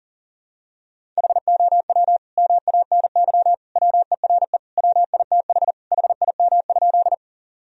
Morse code, heard as a keyed tone, about 40 wpm, 700 Hz. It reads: HOW MANY WERE WITH HIM?
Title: Sentences from Top 1000 Words - Review Part 1 - 40wpm